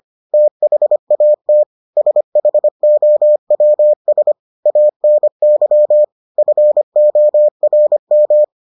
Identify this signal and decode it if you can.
Morse code, heard as a keyed tone, about 25 wpm, 600 Hz. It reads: THAT SHOWS ANY FORM